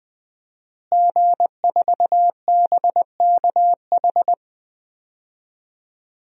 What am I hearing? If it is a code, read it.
Morse code, heard as a keyed tone, about 20 wpm, 700 Hz. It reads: G4BKH
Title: DXCC-100 Call Signs - 5 Characters - Repeated 3x - 20wpm